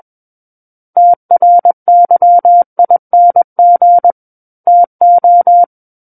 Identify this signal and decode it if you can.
Morse code, heard as a keyed tone, about 21 wpm, 700 Hz. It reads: TRYING TO
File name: Book One / 12-Chapter 12 - 21 WPM